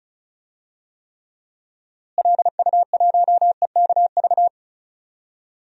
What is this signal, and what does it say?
Morse code, heard as a keyed tone, about 34 wpm, 700 Hz. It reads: LU1EKV
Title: DXCC-100 Call Signs - 6 Characters - Repeated 3x - 35wpm